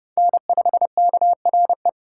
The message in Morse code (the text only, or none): N5KRE